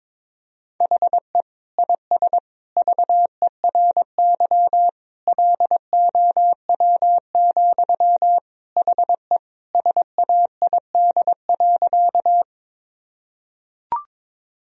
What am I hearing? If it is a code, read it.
Morse code, heard as a keyed tone, about 22 wpm, 700 Hz. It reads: HE IS VERY LOW, HE SAID.
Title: Sentences from Top 1000 Words - Review Part 1 - 22wpm